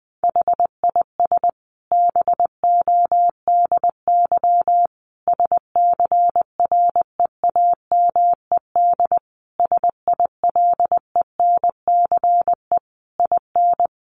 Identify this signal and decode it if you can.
Morse code, heard as a keyed tone, about 20 wpm, 700 Hz. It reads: HIS BODY SCREAMED SILENCE IN
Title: A Sound Of Thunder 20wpm